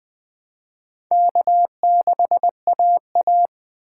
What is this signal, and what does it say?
Morse code, heard as a keyed tone, about 21 wpm, 700 Hz. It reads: K6AA